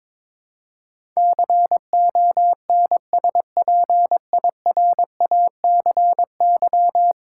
Morse code, heard as a keyed tone, about 21 wpm, 700 Hz. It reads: CONSPIRACY
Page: morse-code-ninja-practice-22wpm